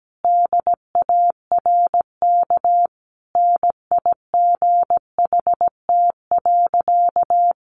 Morse code, heard as a keyed tone, about 17 wpm, 700 Hz. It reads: DARK NIGHT.
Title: Sentences from Top 500 Words 17wpm